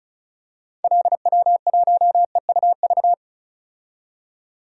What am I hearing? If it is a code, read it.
Morse code, heard as a keyed tone, about 34 wpm, 700 Hz. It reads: LW1EUV
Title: DXCC-100 Call Signs - 6 Characters - 35wpm